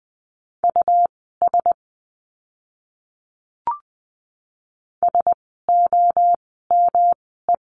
Morse code, heard as a keyed tone, about 20 wpm, spaced as for 14 wpm, 700 Hz. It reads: US SOME
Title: 115 Lesson 19f - Letter w - Words 20wpm